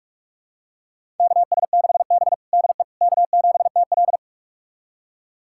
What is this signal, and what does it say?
Morse code, heard as a keyed tone, about 46 wpm, 700 Hz. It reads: KS6B DE K7TL
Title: USA Callsign DE Callsign - 4 Characters - 45wpm